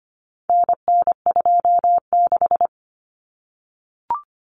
Morse code, heard as a keyed tone, about 25 wpm, 700 Hz. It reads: NN26
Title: Common Cut Numbers - 4-Digit 25wpm